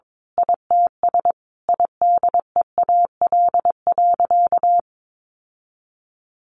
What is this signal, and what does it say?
Morse code, heard as a keyed tone, about 22 wpm, 700 Hz. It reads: ITS IDEAL.
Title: Sentences from Top 1400 Words 22wpm